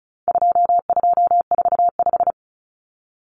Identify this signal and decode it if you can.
Morse code, heard as a keyed tone, about 34 wpm, 700 Hz. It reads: 2245
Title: Numbers - 4-Digit 35wpm